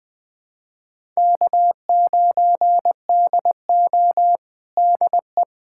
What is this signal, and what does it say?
Morse code, heard as a keyed tone, about 21 wpm, 700 Hz. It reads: K9DO DE
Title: USA Callsign DE Callsign - 4 Characters - 20wpm